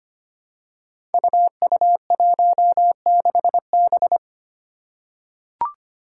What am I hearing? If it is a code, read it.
Morse code, heard as a keyed tone, about 25 wpm, 700 Hz. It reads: UU16B
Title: Five Challenging Letter-Number Combinations 25wpm